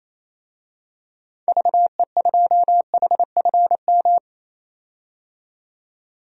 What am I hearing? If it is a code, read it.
Morse code, heard as a keyed tone, about 28 wpm, 700 Hz. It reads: VE2HFM
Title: DXCC-100 Call Signs - 6 Characters - 28wpm